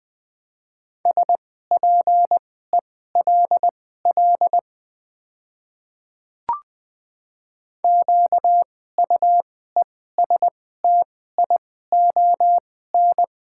Morse code, heard as a keyed tone, about 20 wpm, spaced as for 14 wpm, 700 Hz. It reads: SPELL QUESTION
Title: 2x Character Spacing - Top 500 Words - 20wpm